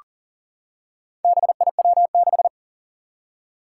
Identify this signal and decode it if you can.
Morse code, heard as a keyed tone, about 41 wpm, 700 Hz. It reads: BIW6
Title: Four Challenging Letter-Number Combinations 40wpm